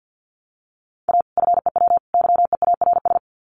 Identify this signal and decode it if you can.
Morse code, heard as a keyed tone, about 49 wpm, 700 Hz. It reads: A FEW YEARS
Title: Sets of 3 Words 50wpm